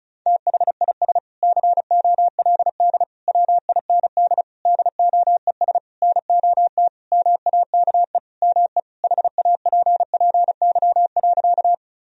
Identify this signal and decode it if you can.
Morse code, heard as a keyed tone, about 36 wpm, 700 Hz. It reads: THIS COLD WIND DOES NOT MAKE ME HAPPY.
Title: Sentences from Top 1000 Words - Review Part 2 - 35wpm